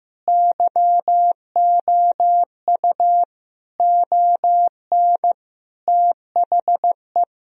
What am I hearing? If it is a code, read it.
Morse code, heard as a keyed tone, about 15 wpm, 700 Hz. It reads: YOU ON THE
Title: Sentences from Top 1000 Words - Review Part 2 - 15wpm